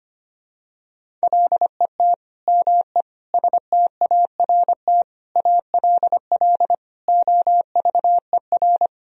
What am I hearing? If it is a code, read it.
Morse code, heard as a keyed tone, about 25 wpm, 700 Hz. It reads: LET ME START ALL OVER
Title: Sentences from Top 1000 Words - Review Part 1 - 25wpm